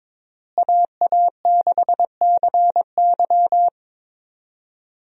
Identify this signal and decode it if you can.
Morse code, heard as a keyed tone, about 22 wpm, 700 Hz. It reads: AA6CY
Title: USA Call Signs - 5 Characters - Repeated 3x - 22wpm